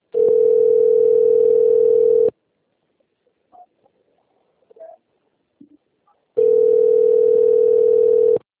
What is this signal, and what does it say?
Ringback tone (North America), 440 + 480 Hz.